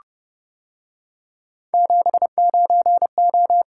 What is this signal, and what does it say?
Morse code, heard as a keyed tone, about 30 wpm, 700 Hz. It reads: 79O